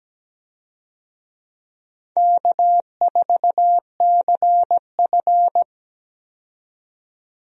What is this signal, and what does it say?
Morse code, heard as a keyed tone, about 17 wpm, 700 Hz. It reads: K4CF
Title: USA Call Signs - 4 Characters - 17wpm